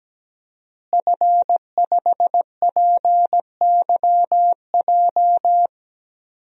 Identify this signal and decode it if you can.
Morse code, heard as a keyed tone, about 17 wpm, 700 Hz. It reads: F5PYJ
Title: DXCC-100 Call Signs - 5 Characters - Repeated 2x - 17wpm